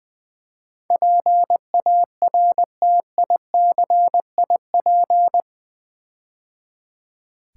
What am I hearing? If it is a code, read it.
Morse code, heard as a keyed tone, about 20 wpm, 700 Hz. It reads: PARTICIP
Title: Build A Word - 4 Letters At A Time - 11-Letter Words - 20wpm